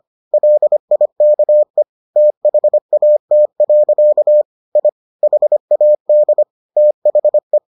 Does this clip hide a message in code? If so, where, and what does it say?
Yes: Morse code from 0.3 to 7.6 s, heard as a keyed tone, about 25 wpm, 600 Hz. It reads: LIKE THAT. I HAD THE